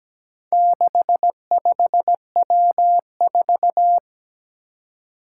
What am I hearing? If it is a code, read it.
Morse code, heard as a keyed tone, about 17 wpm, 700 Hz. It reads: 65W4
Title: Four Challenging Letter-Number Combinations 17wpm